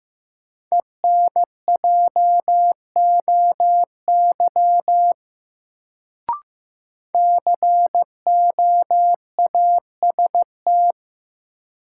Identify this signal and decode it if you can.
Morse code, heard as a keyed tone, about 15 wpm, 700 Hz. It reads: ENJOY COAST
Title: Sentences from Top 1300 Words - New Word Drill Alt - 15wpm